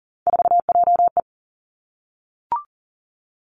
Morse code, heard as a keyed tone, about 42 wpm, 690 Hz. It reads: 4JE
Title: Three Challenging Letter-Number Combinations 40wpm